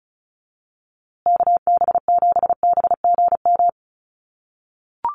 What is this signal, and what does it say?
Morse code, heard as a keyed tone, about 34 wpm, 700 Hz. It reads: KB7BGM
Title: USA Call Signs Repeated 3x 35wpm